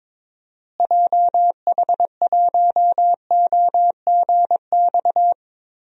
Morse code, heard as a keyed tone, about 22 wpm, 700 Hz. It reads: JH1OGX
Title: DXCC-100 Call Signs - 6 Characters - Repeated 3x - 22wpm